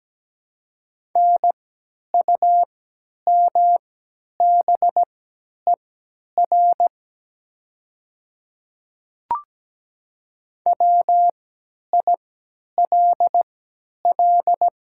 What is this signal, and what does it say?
Morse code, heard as a keyed tone, about 17 wpm, spaced as for 10 wpm, 700 Hz. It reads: NUMBER WILL
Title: 3x Character Spacing - Top 100 Words - 17wpm